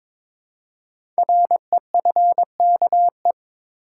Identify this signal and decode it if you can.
Morse code, heard as a keyed tone, about 22 wpm, 690 Hz. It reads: REFKE
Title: Code Groups - 5 Letter:Numbers - 22wpm